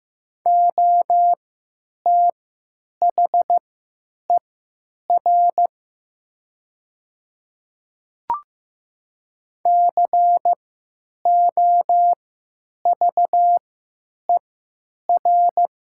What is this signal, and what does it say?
Morse code, heard as a keyed tone, about 15 wpm, spaced as for 9 wpm, 700 Hz. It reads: OTHER COVER
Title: 3x Character Spacing - Top 300 Words - 15wpm